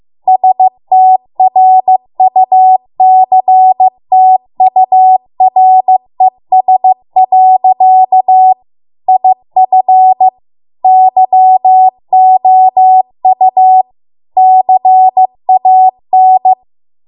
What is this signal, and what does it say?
Morse code, heard as a keyed tone, about 15 wpm, 750 Hz. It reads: STRUCTURES. IF YOU CAN